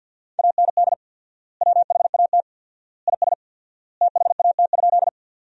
Morse code, heard as a keyed tone, about 50 wpm, 700 Hz. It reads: AND WHAT IS THAT?
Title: 4x Word-Spacing - Sentences from Top 100 Words - 50wpm